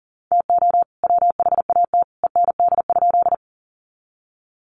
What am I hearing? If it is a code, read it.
Morse code, heard as a keyed tone, about 40 wpm, 700 Hz. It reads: TO WHAT END?
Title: Sentences from Top 200 Words 40wpm